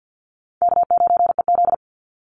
Morse code, heard as a keyed tone, about 50 wpm, 700 Hz. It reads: K9EZ
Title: NA SOTA Chaser Callsigns 50wpm